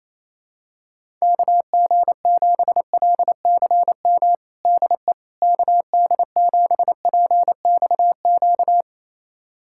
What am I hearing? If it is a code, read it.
Morse code, heard as a keyed tone, about 28 wpm, 700 Hz. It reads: KG7LCM DE KD7PXQ
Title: USA Callsign DE Callsign - 6 Characters - 28wpm